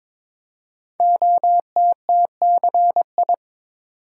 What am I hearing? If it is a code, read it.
Morse code, heard as a keyed tone, about 22 wpm, 700 Hz. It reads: OTTCI
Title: Code Groups - 5 Letters - 22wpm